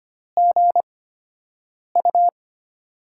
Morse code, heard as a keyed tone, about 25 wpm, spaced as for 7 wpm, 700 Hz. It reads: GU
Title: USA Call Signs - 4 Characters - 8x Character Spacing - 25wpm